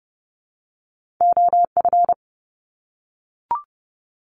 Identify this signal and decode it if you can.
Morse code, heard as a keyed tone, about 30 wpm, 700 Hz. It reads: OF
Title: Top 200 Words 30wpm